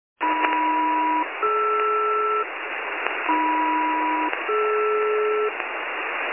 Aircraft SELCAL, heard as two pairs of tones. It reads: AM-DR AM-DR